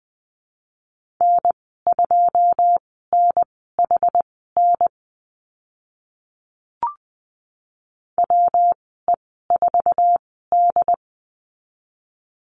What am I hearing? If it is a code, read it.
Morse code, heard as a keyed tone, about 20 wpm, spaced as for 15 wpm, 700 Hz. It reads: N2NHN WE4D
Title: 147 Lesson 24g - Letter y - Call Signs 20wpm